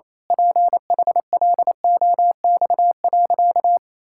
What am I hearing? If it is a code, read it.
Morse code, heard as a keyed tone, about 27 wpm, 700 Hz. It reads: PHLOX.
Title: Pangram Challenge v2 - 28wpm